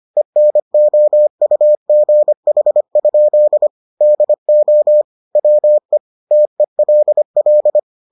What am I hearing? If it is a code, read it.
Morse code, heard as a keyed tone, about 25 wpm, 600 Hz. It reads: ENOUGH? DO WE TELL